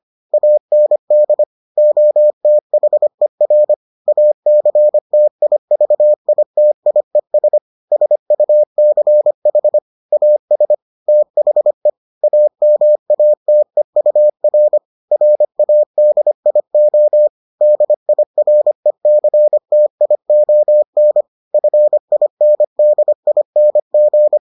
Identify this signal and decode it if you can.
Morse code, heard as a keyed tone, about 25 wpm, 600 Hz. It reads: AND OTHER ACTIVITIES SUCH AS THE AMATEUR RADIO DIRECTION FINDING